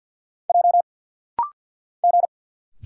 Morse code, heard as a keyed tone, about 50 wpm, 690 Hz. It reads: J G
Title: Single Letters - Rapid-Fire 50wpm